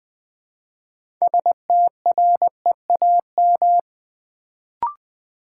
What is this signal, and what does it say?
Morse code, heard as a keyed tone, about 20 wpm, 700 Hz. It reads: STREAM
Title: Build A Word - 2 Letters At A Time - 6-Letter Words - 20wpm